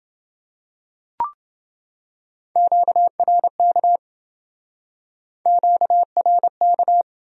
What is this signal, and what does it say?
Morse code, heard as a keyed tone, about 28 wpm, 700 Hz. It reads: QRK QRK